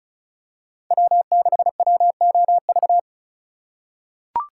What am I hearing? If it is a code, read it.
Morse code, heard as a keyed tone, about 36 wpm, 700 Hz. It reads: W6WOV
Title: USA Call Signs - 5 Characters - 2x Character Spacing - 35wpm